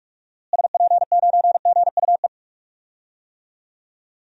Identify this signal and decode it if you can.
Morse code, heard as a keyed tone, about 43 wpm, 700 Hz. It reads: SP9GUE
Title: DXCC-100 Call Signs - 6 Characters - Repeated 3x - 45wpm